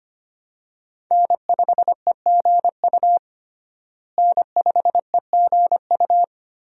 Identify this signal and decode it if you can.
Morse code, heard as a keyed tone, about 25 wpm, 700 Hz. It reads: N5EGU N5EGU
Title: USA Call Signs - 5 Characters - Repeated 3x - 25wpm